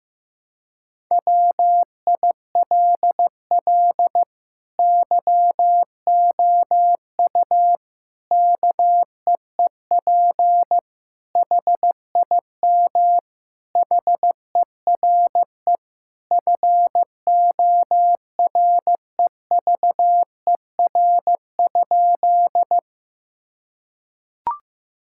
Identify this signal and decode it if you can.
Morse code, heard as a keyed tone, about 16 wpm, 700 Hz. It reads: WILL YOU KEEP HIM HERE FOREVER?
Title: Sentences from Top 1600 Words 15wpm